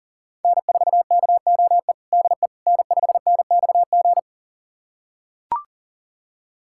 Morse code, heard as a keyed tone, about 40 wpm, 700 Hz. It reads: N4KOE DE N5NXG
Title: USA Callsign DE Callsign - 5 Characters - 40wpm